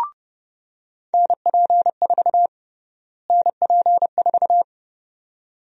Morse code, heard as a keyed tone, about 30 wpm, 700 Hz. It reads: NP4 NP4
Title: USA Call Sign Prefixes - Repeated 2x - 30wpm